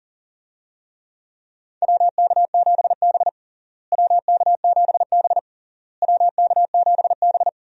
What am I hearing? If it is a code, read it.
Morse code, heard as a keyed tone, about 40 wpm, 700 Hz. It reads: WK7B WK7B WK7B
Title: USA Call Signs - 4 Characters - Repeated 3x - 40wpm